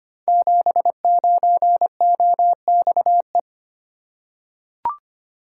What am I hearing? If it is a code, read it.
Morse code, heard as a keyed tone, about 25 wpm, 700 Hz. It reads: 79OXE
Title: Code Groups - 5 Letter:Numbers - 25wpm